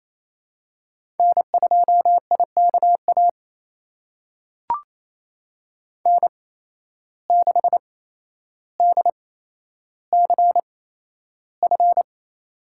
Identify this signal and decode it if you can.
Morse code, heard as a keyed tone, about 28 wpm, spaced as for 12 wpm, 700 Hz. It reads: N2IKA N6DCF
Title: USA Call Signs - 5 Characters - 8x Character Spacing - 28wpm